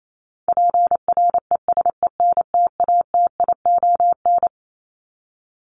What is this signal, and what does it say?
Morse code, heard as a keyed tone, about 28 wpm, 700 Hz. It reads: PRESENTATION
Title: Top 1501-1750 Words 28wpm